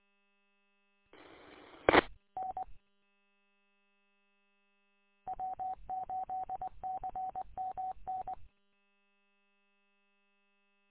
Morse code, heard as a keyed tone, about 20 wpm, 750 Hz. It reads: I N W8CMN
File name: w8cmn- -combined